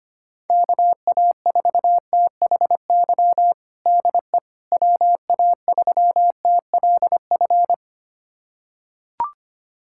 Morse code, heard as a keyed tone, about 25 wpm, 700 Hz. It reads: KA4THY DE WA3TLF
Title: USA Callsign DE Callsign 25wpm